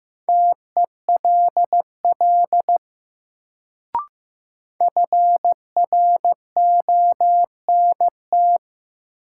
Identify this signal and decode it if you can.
Morse code, heard as a keyed tone, about 15 wpm, 700 Hz. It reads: TELL FRONT
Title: Sets of 2 Words - New Word Drill - 15wpm